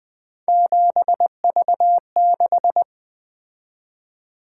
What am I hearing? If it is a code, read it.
Morse code, heard as a keyed tone, about 20 wpm, 700 Hz. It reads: 7V6
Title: Three Letter-Number Combinations - 20wpm